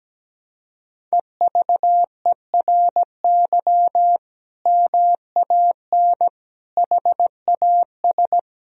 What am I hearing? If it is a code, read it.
Morse code, heard as a keyed tone, about 17 wpm, 700 Hz. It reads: EVERY MAN HAS
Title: Sentences from Top 1000 Words - Review Part 4 - 17wpm